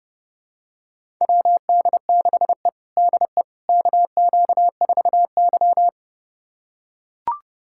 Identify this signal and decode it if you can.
Morse code, heard as a keyed tone, about 30 wpm, 700 Hz. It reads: WD6E DE KQ4Y
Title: USA Callsign DE Callsign - 4 Characters - 30wpm